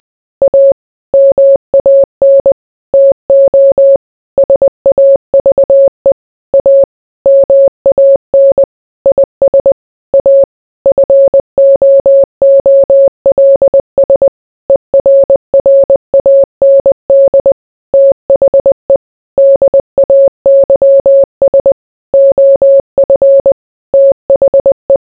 Morse code, heard as a keyed tone, about 20 wpm, 550 Hz. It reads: A MAN TO SAVE A MAN IS A FOOLS ERRAND THE DAYS OF THE